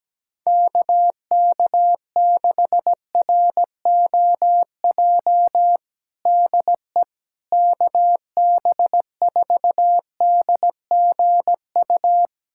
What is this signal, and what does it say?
Morse code, heard as a keyed tone, about 17 wpm, 700 Hz. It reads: KK6ROJ DE KB4DGU
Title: USA Callsign DE Callsign - 6 Characters - 17wpm